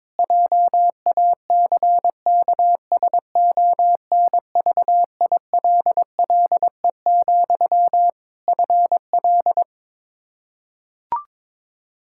Morse code, heard as a keyed tone, about 22 wpm, 700 Hz. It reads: JACKSONVILLE, FL
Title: Practice QSO - Exchange 1 - Beginning - 22wpm